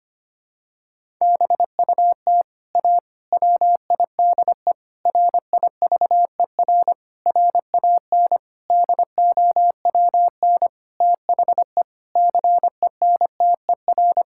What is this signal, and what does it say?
Morse code, heard as a keyed tone, about 25 wpm, 700 Hz. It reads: BUT A WIDE RIVER RAN DOWN THE CENTER
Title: Sentences from Top 1000 Words - Review Part 6 - 25wpm